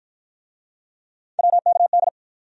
Morse code, heard as a keyed tone, about 53 wpm, 700 Hz. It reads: WKD